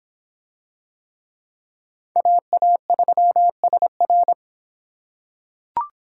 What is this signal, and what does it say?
Morse code, heard as a keyed tone, about 26 wpm, 700 Hz. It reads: AA3SR